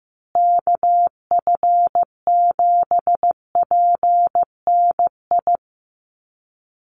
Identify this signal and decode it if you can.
Morse code, heard as a keyed tone, about 15 wpm, 700 Hz. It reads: KF7PNI
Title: USA Call Signs Repeated 3x 15wpm